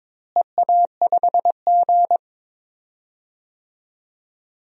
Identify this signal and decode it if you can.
Morse code, heard as a keyed tone, about 22 wpm, 700 Hz. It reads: EA5G
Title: DXCC-100 Call Signs - 4 Characters - Repeated 2x - 22wpm